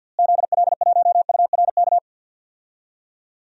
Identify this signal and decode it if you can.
Morse code, heard as a keyed tone, about 52 wpm, 700 Hz. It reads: ZL1URK
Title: DXCC-100 Call Signs - 6 Characters - Repeated 3x - 50wpm